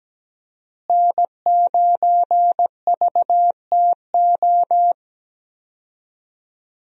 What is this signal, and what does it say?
Morse code, heard as a keyed tone, about 17 wpm, 700 Hz. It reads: N9VTO
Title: USA Call Signs - 5 Characters - Repeated 3x - 17wpm